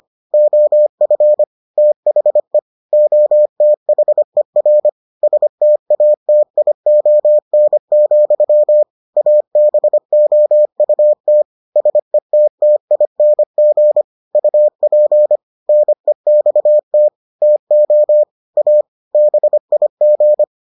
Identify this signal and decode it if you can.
Morse code, heard as a keyed tone, about 25 wpm, 600 Hz. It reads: OF THE OTHER STATION, ABOUT SETTING UP NEXT TO A BIG